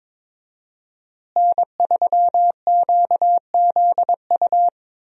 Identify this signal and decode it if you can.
Morse code, heard as a keyed tone, about 21 wpm, 700 Hz. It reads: N3QZU
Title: USA Call Signs - 5 Characters - Repeated 3x - 22wpm